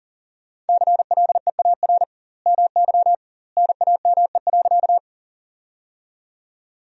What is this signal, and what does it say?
Morse code, heard as a keyed tone, about 40 wpm, 700 Hz. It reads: CLEAR MY NAME.
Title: 2x Word-Spacing - Sentences from Top 500 Words - 40wpm